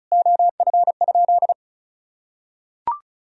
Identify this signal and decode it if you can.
Morse code, heard as a keyed tone, about 36 wpm, 700 Hz. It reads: OF?
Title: Sentences from Top 200 Words and Abbreviations 35wpm